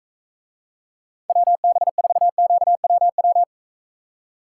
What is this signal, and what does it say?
Morse code, heard as a keyed tone, about 45 wpm, 700 Hz. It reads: WB4QWW